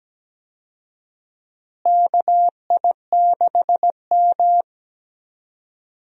Morse code, heard as a keyed tone, about 17 wpm, 700 Hz. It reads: KI6M
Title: USA Call Signs - 4 Characters - Repeated 2x - 17wpm